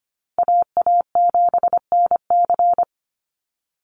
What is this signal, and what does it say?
Morse code, heard as a keyed tone, about 25 wpm, 700 Hz. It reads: AA7NC
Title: USA Call Signs Repeated 3x 25wpm